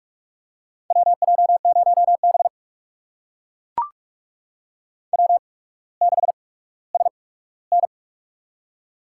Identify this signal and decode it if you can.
Morse code, heard as a keyed tone, about 42 wpm, spaced as for 19 wpm, 700 Hz. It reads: WJ0B W6SN
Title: USA Call Signs - 4 Characters - 8x Character Spacing - 45wpm